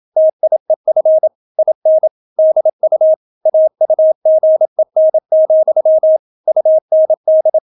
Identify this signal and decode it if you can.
Morse code, heard as a keyed tone, about 27 wpm, 630 Hz. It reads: TIEF IN DU AUGEN, UND